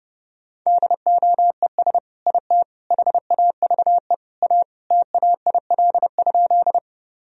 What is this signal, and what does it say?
Morse code, heard as a keyed tone, about 30 wpm, 700 Hz. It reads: DOES IT HAVE A TAIL?
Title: Sentences from Top 1000 Words - Review Part 3 - 30wpm